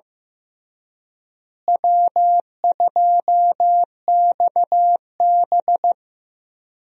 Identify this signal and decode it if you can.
Morse code, heard as a keyed tone, about 15 wpm, 700 Hz. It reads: W2XB
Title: USA Call Signs - 4 Characters - Repeated 3x - 15wpm